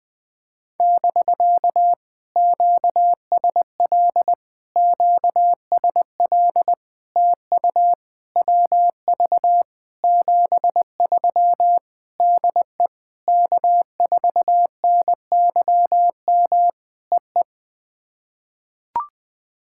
Morse code, heard as a keyed tone, about 20 wpm, 700 Hz. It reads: <BK> QSL QSL TU WV 73 DE K4NYM EE
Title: POTA QSO Practice - 20wpm